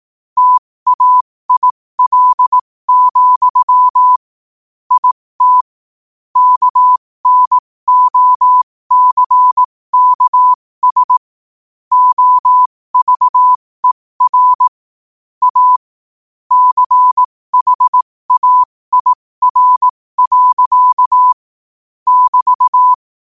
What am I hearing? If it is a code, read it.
Morse code, heard as a keyed tone, about 18 wpm, spaced as for 15 wpm, 1000 Hz. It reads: TAIL, IT KNOCKS OVER A CHAIR. <BT>
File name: qotdfiles / 15wpm